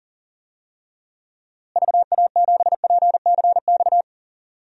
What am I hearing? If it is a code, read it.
Morse code, heard as a keyed tone, about 40 wpm, 700 Hz. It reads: VA7PCX
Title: DXCC-100 Call Signs - 6 Characters - Repeated 3x - 40wpm